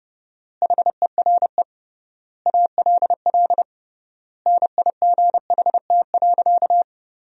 Morse code, heard as a keyed tone, about 32 wpm, 700 Hz. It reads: HERE ALL NIGHT.
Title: 3x Word-Spacing - Sentences from Top 500 Words - 30wpm